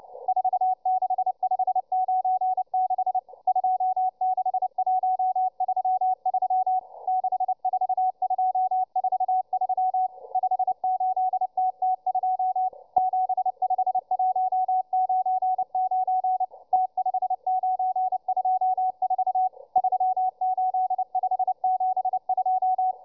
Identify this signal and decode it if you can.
Morse code, heard as a keyed tone, about 29 wpm, 740 Hz, with radio noise. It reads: 46596 26133 64243 58TT2 75199 T5924 38572